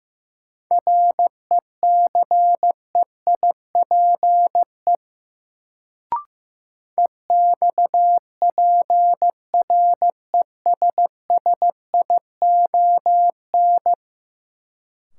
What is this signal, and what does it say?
Morse code, heard as a keyed tone, about 15 wpm, 700 Hz. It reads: RECEIPE EXPRESSION
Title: Sentences from Top 1200 Words - New Word Drill - 15wpm